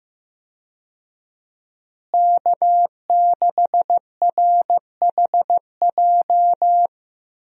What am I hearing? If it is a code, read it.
Morse code, heard as a keyed tone, about 15 wpm, 700 Hz. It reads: K6RHJ
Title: USA Call Signs - 5 Characters - Repeated 2x - 15wpm